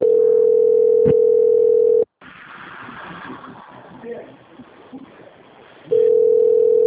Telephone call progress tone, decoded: ringback tone

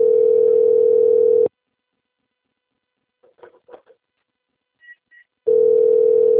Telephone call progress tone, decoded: ringback tone

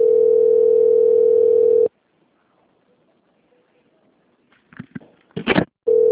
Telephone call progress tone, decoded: ringback tone

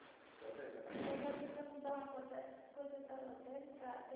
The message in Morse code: none

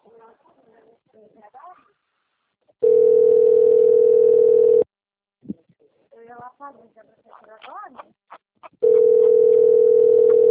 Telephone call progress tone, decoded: ringback tone

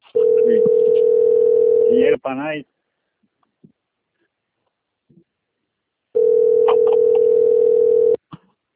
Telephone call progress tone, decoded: ringback tone

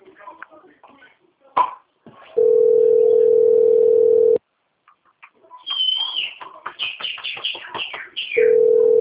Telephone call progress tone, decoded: ringback tone

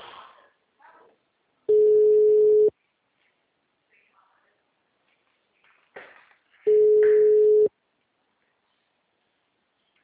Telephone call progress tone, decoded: ringback tone

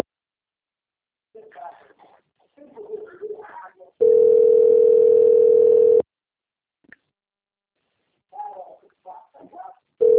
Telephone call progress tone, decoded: ringback tone

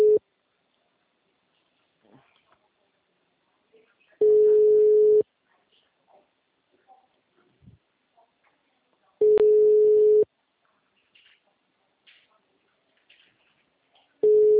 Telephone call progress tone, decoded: ringback tone

